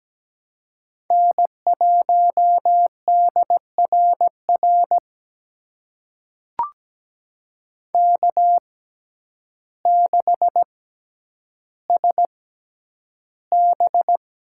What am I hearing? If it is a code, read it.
Morse code, heard as a keyed tone, about 17 wpm, spaced as for 9 wpm, 700 Hz. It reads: N1DRR K6SB
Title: USA Call Signs - 6x Character Spacing - 17wpm